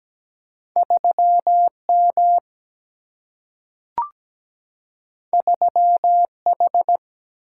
Morse code, heard as a keyed tone, about 17 wpm, 700 Hz. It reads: 3M 3H
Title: Single-Letter Single-Number 17wpm